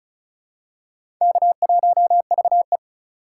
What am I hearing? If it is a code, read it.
Morse code, heard as a keyed tone, about 36 wpm, 700 Hz. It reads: K1VE
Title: USA Call Signs - 3x Character Spacing - 35wpm